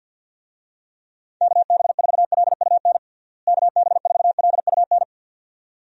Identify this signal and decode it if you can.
Morse code, heard as a keyed tone, about 53 wpm, 700 Hz. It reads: KB4LUN KB4LUN
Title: USA Call Signs - 6 Characters - Repeated 2x - 50wpm